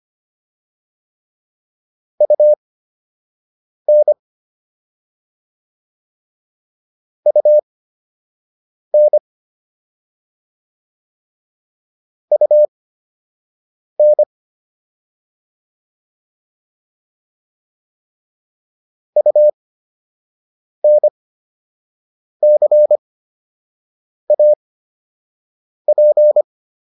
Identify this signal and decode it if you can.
Morse code, heard as a keyed tone, about 25 wpm, spaced as for 6 wpm, 600 Hz. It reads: UN UN UN UNCAP
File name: UN5 06